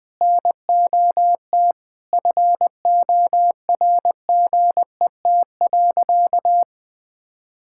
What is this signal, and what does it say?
Morse code, heard as a keyed tone, about 20 wpm, 700 Hz. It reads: NOT FORGET.
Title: Sentences with 4-Word Phrases - 20wpm